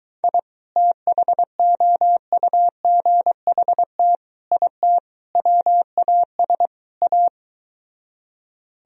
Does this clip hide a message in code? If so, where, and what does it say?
Yes: Morse code from 0.2 to 7.3 s, heard as a keyed tone, about 23 wpm, 700 Hz. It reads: I THOUGHT IT WAS A